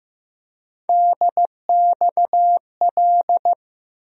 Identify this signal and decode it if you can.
Morse code, heard as a keyed tone, about 15 wpm, 700 Hz. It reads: DXL
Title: USA Call Sign Postfixes - 3 Letters 15wpm